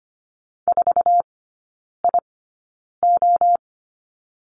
Morse code, heard as a keyed tone, about 25 wpm, spaced as for 9 wpm, 700 Hz. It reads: 4IO
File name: Lesson 14e - Number 2 - Three Characters 25wpm